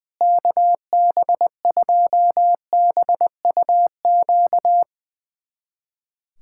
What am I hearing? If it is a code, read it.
Morse code, heard as a keyed tone, about 20 wpm, 700 Hz. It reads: KB2BUQ
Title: USA Callsign DE Callsign - 6 Characters - 20wpm